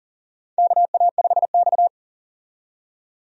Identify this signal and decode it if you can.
Morse code, heard as a keyed tone, about 40 wpm, 700 Hz. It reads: KA5X
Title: USA Call Signs - 4 Characters - 40wpm